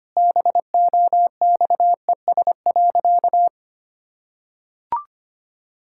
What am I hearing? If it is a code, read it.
Morse code, heard as a keyed tone, about 25 wpm, 700 Hz. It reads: BOXES.
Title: Pangram Challenge v2 - 25wpm